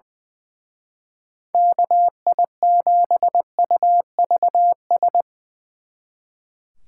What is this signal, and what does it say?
Morse code, heard as a keyed tone, about 23 wpm, 700 Hz. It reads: KI7UVS
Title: Speed Racing - USA 2x3 Call Signs 20wpm